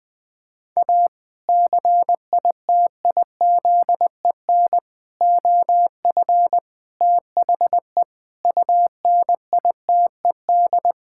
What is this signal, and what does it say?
Morse code, heard as a keyed tone, about 21 wpm, 700 Hz. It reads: A CITIZEN OF THE UNITED
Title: Sentences with 3-Word Phrases - 20wpm